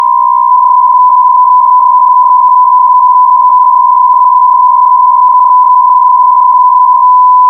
PSK63, about 62.5 baud.